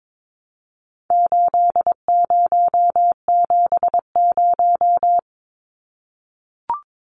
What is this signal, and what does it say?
Morse code, heard as a keyed tone, about 22 wpm, 700 Hz. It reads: 8070
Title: Numbers - 4-Digit 22wpm